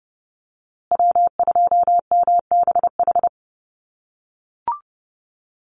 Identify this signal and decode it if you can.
Morse code, heard as a keyed tone, about 30 wpm, 700 Hz. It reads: W2MBH
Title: USA Call Signs Repeated 2x 30wpm